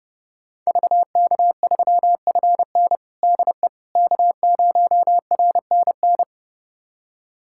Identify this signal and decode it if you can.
Morse code, heard as a keyed tone, about 30 wpm, 700 Hz. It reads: VK3FN DE K0RNN
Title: USA Callsign DE Callsign - 5 Characters - 30wpm